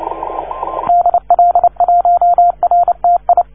Morse code, heard as a keyed tone, about 30 wpm, 690 Hz, with radio noise. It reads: DL1RTI
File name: 1esempio-telegrafia